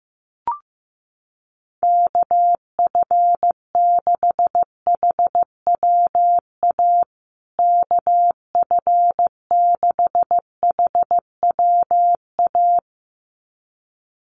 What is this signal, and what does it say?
Morse code, heard as a keyed tone, about 15 wpm, 700 Hz. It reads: KF6HWA KF6HWA
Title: USA Call Signs Repeated 2x 15wpm